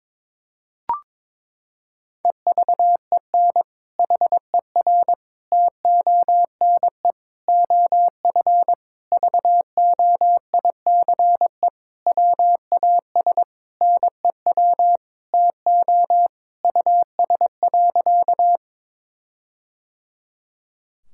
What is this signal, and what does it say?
Morse code, heard as a keyed tone, about 22 wpm, 700 Hz. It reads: EVEN HER TONE OF VOICE WAS NEW TO US.
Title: Sentences from Top 1600 Words 22wpm